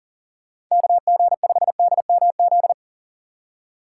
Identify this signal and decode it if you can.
Morse code, heard as a keyed tone, about 40 wpm, 700 Hz. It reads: KG5DMZ
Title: USA Call Signs - 6 Characters - 4x Character Spacing - 40wpm